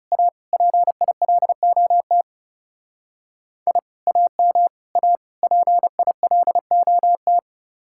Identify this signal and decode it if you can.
Morse code, heard as a keyed tone, about 36 wpm, 700 Hz. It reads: A PILOT I AM A PILOT
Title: Speed-Racing Common QSO Phrases - 30wpm